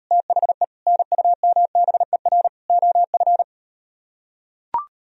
Morse code, heard as a keyed tone, about 38 wpm, 700 Hz. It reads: THE NUMBER OF